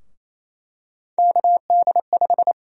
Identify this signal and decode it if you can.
Morse code, heard as a keyed tone, about 28 wpm, 700 Hz. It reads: KD5
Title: USA Call Sign Prefixes - Repeated 2x - 28wpm